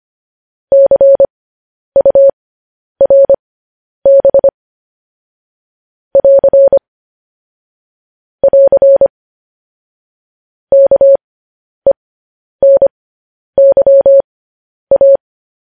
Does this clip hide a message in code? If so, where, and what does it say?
Yes: Morse code from 0.7 to 15.2 s, heard as a keyed tone, about 25 wpm, spaced as for 10 wpm, 550 Hz. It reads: CURB <AR> <AR> KENYA